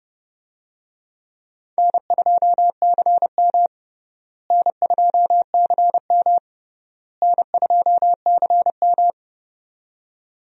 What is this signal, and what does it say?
Morse code, heard as a keyed tone, about 30 wpm, 700 Hz. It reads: N2CM N2CM N2CM
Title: USA Call Signs - 4 Characters - Repeated 3x - 30wpm